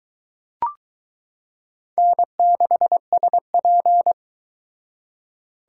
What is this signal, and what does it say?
Morse code, heard as a keyed tone, about 23 wpm, 700 Hz. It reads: N6SP